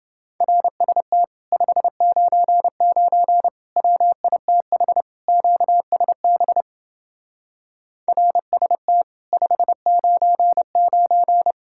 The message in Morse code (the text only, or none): RST 599 WITH QSB RST 599